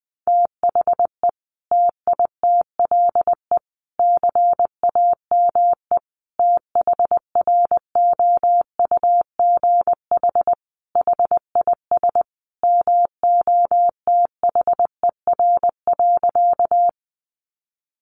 Morse code, heard as a keyed tone, about 20 wpm, 700 Hz. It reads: THE TITLE CAME THROUGH HIS MOTHER.
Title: Sentences from Top 1200 Words 20wpm